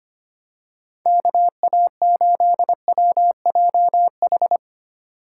Morse code, heard as a keyed tone, about 25 wpm, 700 Hz. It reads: KA8WJH
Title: USA Call Signs - 4x Character Spacing - 25wpm